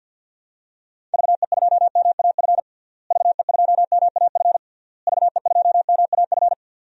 Morse code, heard as a keyed tone, about 50 wpm, 700 Hz. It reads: VE2MAF VE2MAF VE2MAF
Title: DXCC-100 Call Signs - 6 Characters - Repeated 3x - 50wpm